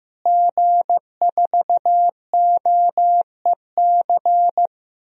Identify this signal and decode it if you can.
Morse code, heard as a keyed tone, about 15 wpm, 700 Hz. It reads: G4OEC
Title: DXCC-100 Call Signs - 5 Characters - Repeated 2x - 15wpm